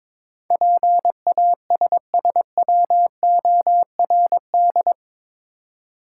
Morse code, heard as a keyed tone, about 22 wpm, 700 Hz. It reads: PASSWORD
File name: practice / 22wpm / Eight Letter Words - 22wpm